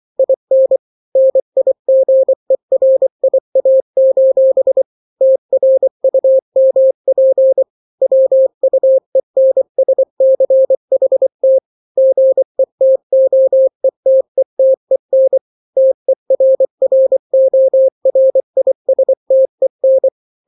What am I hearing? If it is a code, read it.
Morse code, heard as a keyed tone, about 24 wpm, 530 Hz. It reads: IN NIGERIA: TRUMP WUENSCHT GETOETETEN TERRORISTEN
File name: daily / 23wpm